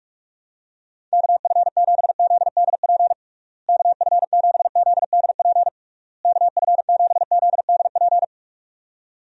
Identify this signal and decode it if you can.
Morse code, heard as a keyed tone, about 46 wpm, 700 Hz. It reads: KF7ZDP KF7ZDP KF7ZDP
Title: USA Call Signs - 6 Characters - Repeated 3x - 45wpm